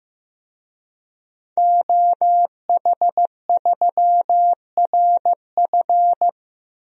Morse code, heard as a keyed tone, about 15 wpm, 700 Hz. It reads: OH3RF
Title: DXCC-100 Call Signs - 5 Characters - Repeated 2x - 15wpm